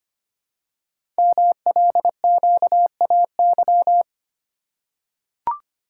Morse code, heard as a keyed tone, about 25 wpm, 700 Hz. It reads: MLQAY
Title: Code Groups - 5 Letters - 25wpm